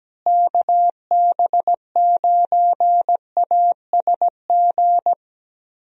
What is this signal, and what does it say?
Morse code, heard as a keyed tone, about 17 wpm, 700 Hz. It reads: KB9ASG
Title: USA Call Signs - 6 Characters - Repeated 3x - 17wpm